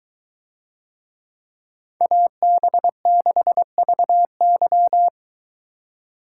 Morse code, heard as a keyed tone, about 21 wpm, 700 Hz. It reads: AB6VY